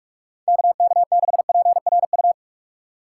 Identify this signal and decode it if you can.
Morse code, heard as a keyed tone, about 55 wpm, 700 Hz. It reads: KK6PRU
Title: Speed Racing - USA 2x3 Call Signs 45wpm